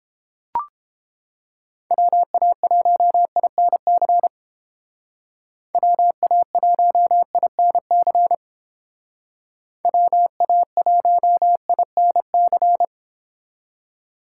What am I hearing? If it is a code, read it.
Morse code, heard as a keyed tone, about 30 wpm, 700 Hz. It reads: WA1INC WA1INC WA1INC